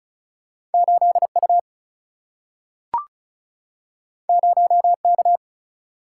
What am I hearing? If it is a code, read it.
Morse code, heard as a keyed tone, about 36 wpm, 700 Hz. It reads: 8U 0K
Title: Single-Letter Single-Number 35wpm